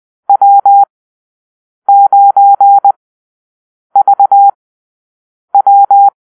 Morse code, heard as a keyed tone, about 20 wpm, spaced as for 7 wpm, 800 Hz. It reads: W9VW
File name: arrl / Call-Sign Drill-8